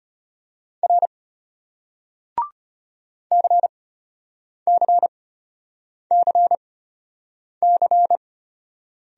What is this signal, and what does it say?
Morse code, heard as a keyed tone, about 33 wpm, spaced as for 8 wpm, 700 Hz. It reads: R CCCC